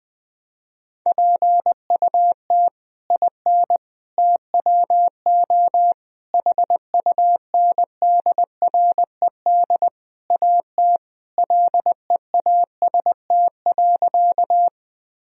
Morse code, heard as a keyed tone, about 21 wpm, 700 Hz. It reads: PUT IN TWO HUNDRED AT LEAST.
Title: Sentences from Top 1000 Words - Review Part 2 - 20wpm